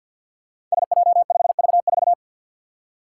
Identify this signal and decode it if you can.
Morse code, heard as a keyed tone, about 48 wpm, 700 Hz. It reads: SJ5V4
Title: Five Challenging Letter-Number Combinations 50wpm